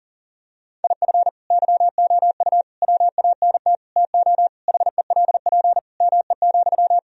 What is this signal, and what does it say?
Morse code, heard as a keyed tone, about 40 wpm, 700 Hz. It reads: IF YOU WANT TO HELP ME,
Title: Sentences from Top 1000 Words - Review Part 1 - 40wpm